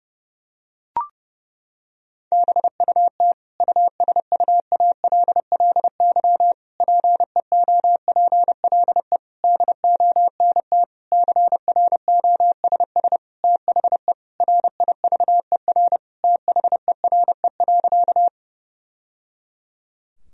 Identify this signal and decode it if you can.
Morse code, heard as a keyed tone, about 32 wpm, 700 Hz. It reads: BUT USUALLY PEOPLE DONT CROSS THE RIVER THERE.
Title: Sentences from Top 1000 Words - Review Part 3 - 30wpm